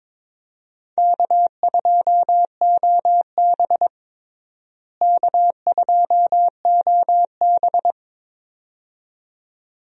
Morse code, heard as a keyed tone, about 22 wpm, 700 Hz. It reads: K2OB K2OB
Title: USA Call Signs - 4 Characters - Repeated 2x - 22wpm